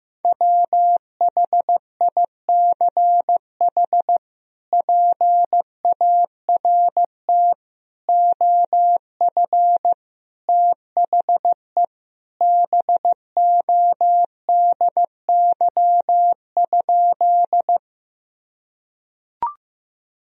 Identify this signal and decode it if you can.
Morse code, heard as a keyed tone, about 15 wpm, 700 Hz. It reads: WHICH PART OF THE BODY?
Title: Sentences from Top 1000 Words - Review Part 2 - 15wpm